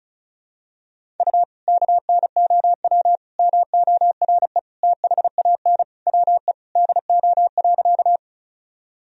Morse code, heard as a keyed tone, about 36 wpm, 700 Hz. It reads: U KNOW MORE THAN WE DO.
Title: Sentences from Top 100 Words and Abbreviations 35wpm